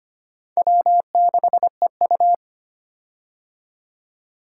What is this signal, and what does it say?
Morse code, heard as a keyed tone, about 25 wpm, 700 Hz. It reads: W6EU
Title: K1USN SST QSO Practice - 25wpm